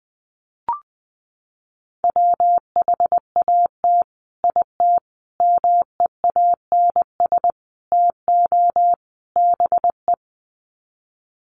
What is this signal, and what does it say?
Morse code, heard as a keyed tone, about 20 wpm, 700 Hz. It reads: WHAT IT MEANS TO BE
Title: Sets of 5 Words 20wpm